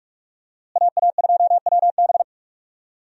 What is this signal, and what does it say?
Morse code, heard as a keyed tone, about 44 wpm, 700 Hz. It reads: AA2WB
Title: USA Call Signs - 5 Characters - Repeated 3x - 45wpm